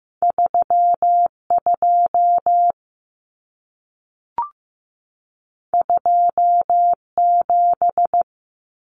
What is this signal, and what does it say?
Morse code, heard as a keyed tone, about 15 wpm, 700 Hz. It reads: 32 27
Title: Common Cut Numbers - 2-Digit 15wpm